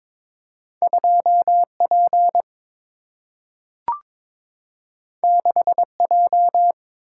Morse code, heard as a keyed tone, about 22 wpm, 700 Hz. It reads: 2 P 6 J
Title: Single-Letter Single-Number 22wpm